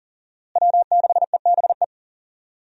Morse code, heard as a keyed tone, about 39 wpm, 700 Hz. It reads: W6EBE
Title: USA Call Signs - 5 Characters - Repeated 2x - 40wpm